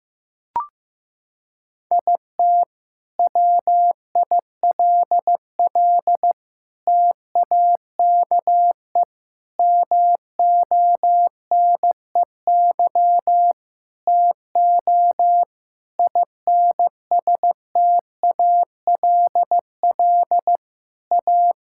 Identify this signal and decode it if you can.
Morse code, heard as a keyed tone, about 15 wpm, 700 Hz. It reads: IT WILL TAKE MONEY TO INSTALL A
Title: Sentences from Top 1600 Words 15wpm